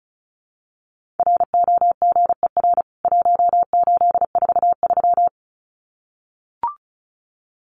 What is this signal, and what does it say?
Morse code, heard as a keyed tone, about 34 wpm, 700 Hz. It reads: ROGER 1843
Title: CWT Exchange 35wpm